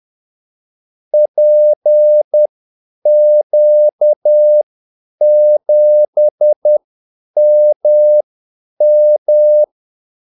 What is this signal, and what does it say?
Morse code, heard as a keyed tone, about 10 wpm, spaced as for 8 wpm, 600 Hz. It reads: PQ7MM
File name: Callsigns-10-8wpm